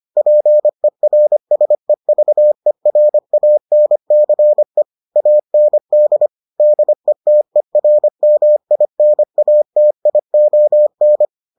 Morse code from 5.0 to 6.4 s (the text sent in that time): AND